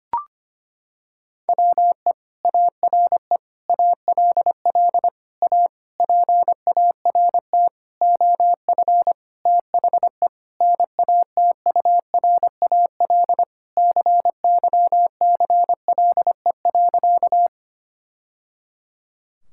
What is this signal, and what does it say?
Morse code, heard as a keyed tone, about 25 wpm, 700 Hz. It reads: WE ARE ALL A PART OF THE NATURAL CYCLE.
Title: Sentences from Top 1000 Words - Review Part 6 - 25wpm